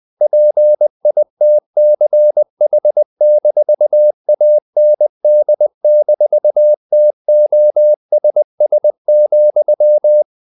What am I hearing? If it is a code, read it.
Morse code, heard as a keyed tone, about 20 wpm, 600 Hz. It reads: PITCH-AND-TOSS,
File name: CW / Practice / If20